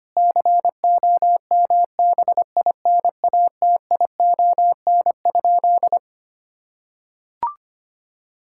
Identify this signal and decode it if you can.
Morse code, heard as a keyed tone, about 25 wpm, 700 Hz. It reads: COMBINATION?
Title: Sentences from Top 1000 Words - Review Part 6 - 25wpm